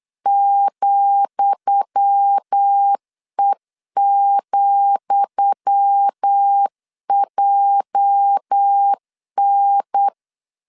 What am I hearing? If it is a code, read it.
Morse code, heard as a keyed tone, about 8 wpm, 800 Hz. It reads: ,E,JN
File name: LEZIONI / 10 SIGNS / 10D COMA 8wpm 3749